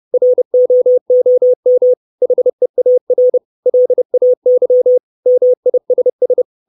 Morse code, heard as a keyed tone, about 30 wpm, 500 Hz. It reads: ROOM HEAR LAY MISS
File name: 632 Words 30WPM